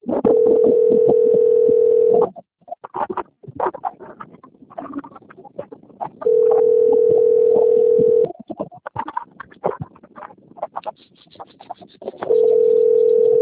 Ringback tone (North America), 440 + 480 Hz.